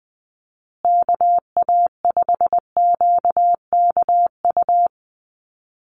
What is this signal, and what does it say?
Morse code, heard as a keyed tone, about 20 wpm, 700 Hz. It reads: KA5QKU
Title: USA Call Signs Repeated 2x 20wpm